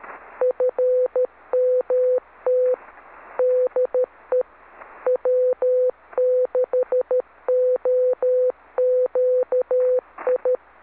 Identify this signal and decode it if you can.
Morse code, heard as a keyed tone, about 13 wpm, 500 Hz. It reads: FMT DE W6OQI